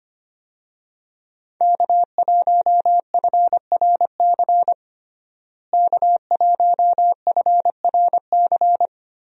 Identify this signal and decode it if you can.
Morse code, heard as a keyed tone, about 25 wpm, 700 Hz. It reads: K1FRC K1FRC
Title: USA Call Signs - 5 Characters - Repeated 2x - 25wpm